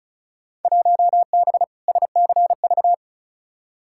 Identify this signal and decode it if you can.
Morse code, heard as a keyed tone, about 36 wpm, 700 Hz. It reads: 1B SCV
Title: Field Day Exchanges - 35wpm